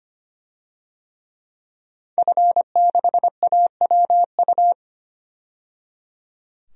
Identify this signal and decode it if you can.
Morse code, heard as a keyed tone, about 25 wpm, 700 Hz. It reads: F6AWU
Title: DXCC-100 Call Signs - 5 Characters - 25wpm